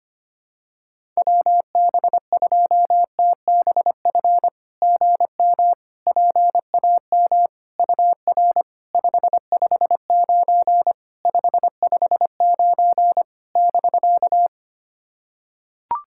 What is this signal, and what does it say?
Morse code, heard as a keyed tone, about 25 wpm, 690 Hz. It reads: WB2TBF GM PAM UR 559 559 <BK>
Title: SOTA QSO Practice 25wpm